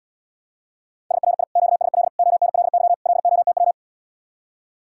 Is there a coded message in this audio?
Yes: Morse code from 1.1 to 3.7 s, heard as a keyed tone, about 80 wpm, 700 Hz. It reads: SRI QSL ONLY LOTW